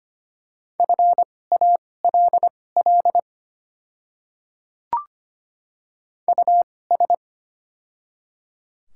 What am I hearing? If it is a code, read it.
Morse code, heard as a keyed tone, about 25 wpm, spaced as for 18 wpm, 700 Hz. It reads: FALL US
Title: 2x Character Spacing - Top 500 Words - 25wpm